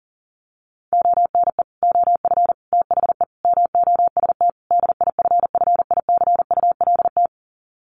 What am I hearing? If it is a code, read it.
Morse code, heard as a keyed tone, about 40 wpm, 700 Hz. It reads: ONE OF THE MOST DIFFICULT
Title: Sets of 5 Words Encore2 40wpm